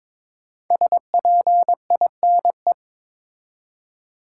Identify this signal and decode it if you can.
Morse code, heard as a keyed tone, about 22 wpm, 700 Hz. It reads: SPINE
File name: 5 Letter Words - First 3 Letters Given - 22wpm